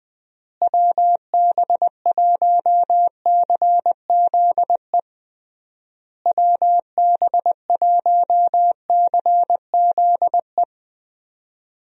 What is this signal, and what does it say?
Morse code, heard as a keyed tone, about 20 wpm, 700 Hz. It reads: WB1CZE WB1CZE
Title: USA Call Signs - 6 Characters - Repeated 3x - 20wpm